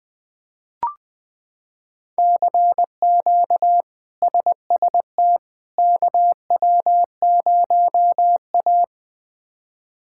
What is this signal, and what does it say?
Morse code, heard as a keyed tone, about 21 wpm, 700 Hz. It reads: CQ SST KW0A
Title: K1USN SST QSO Practice - 20wpm